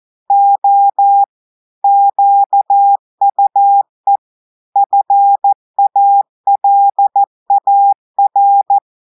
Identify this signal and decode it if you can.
Morse code, heard as a keyed tone, about 14 wpm, 800 Hz. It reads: O QUE FALAR
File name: audios / portugais / portugais texte en clair 2 vitesse 14 mots minutes